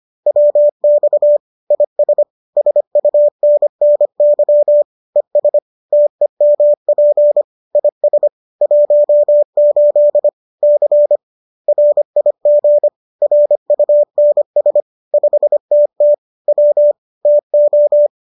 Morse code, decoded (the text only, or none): WX IS SUNNY ES TEMP IS 18 C RIG RUNS 5TT W TO